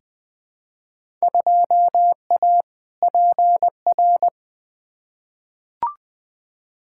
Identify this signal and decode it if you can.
Morse code, heard as a keyed tone, about 20 wpm, 700 Hz. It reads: 2A PR
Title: Field Day Exchanges Encore - 20wpm